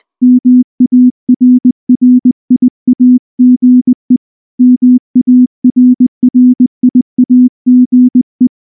Morse code, heard as a keyed tone, about 19 wpm, 250 Hz. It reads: MARRIAGE MARRIAGE
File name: morse / morse-training-20WPM-7